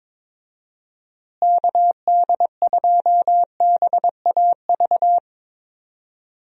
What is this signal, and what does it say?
Morse code, heard as a keyed tone, about 22 wpm, 700 Hz. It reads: KD2BAV
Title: USA Call Signs - 6 Characters - Repeated 3x - 22wpm